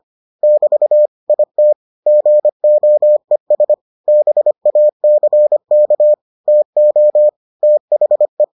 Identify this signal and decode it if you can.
Morse code, heard as a keyed tone, about 25 wpm, 600 Hz. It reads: <BT> IT GOES BACK TO THE